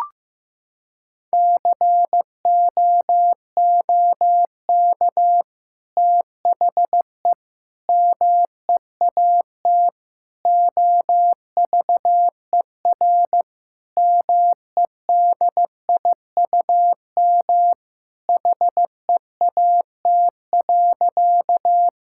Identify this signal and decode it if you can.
Morse code, heard as a keyed tone, about 15 wpm, 700 Hz. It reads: COOK THE MEAT OVER MEDIUM HEAT.
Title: Sentences from Top 1000 Words - Review Part 6 - 15wpm